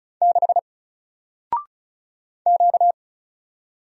Morse code, heard as a keyed tone, about 35 wpm, 700 Hz. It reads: 6 Q
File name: Single-Letter-Number-35wpm